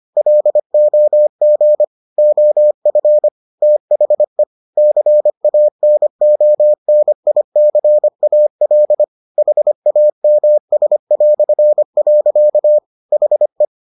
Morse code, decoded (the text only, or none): LOG OF THE CANONICAL HAMS". HE